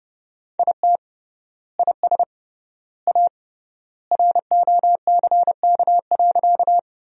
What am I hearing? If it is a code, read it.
Morse code, heard as a keyed tone, about 30 wpm, 700 Hz. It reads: IT IS A ROCK.
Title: 3x Word-Spacing - Sentences from Top 400 Words - 30wpm